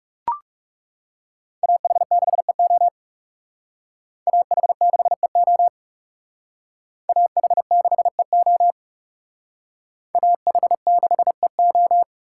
Morse code, heard as a keyed tone, about 39 wpm, 700 Hz. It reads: AH6EO AH6EO AH6EO AH6EO